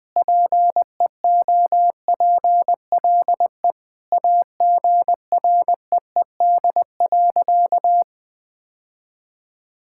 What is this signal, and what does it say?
Morse code, heard as a keyed tone, about 21 wpm, 700 Hz. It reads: PEOPLE AGREED.
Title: Sentences with 3-Word Phrases - Spoken Beginning - 20wpm